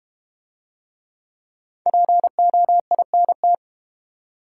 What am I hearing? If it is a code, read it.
Morse code, heard as a keyed tone, about 32 wpm, 700 Hz. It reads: POINT